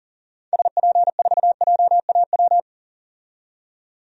Morse code, heard as a keyed tone, about 37 wpm, 700 Hz. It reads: SP4JAW
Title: DXCC-100 Call Signs - 6 Characters - 40wpm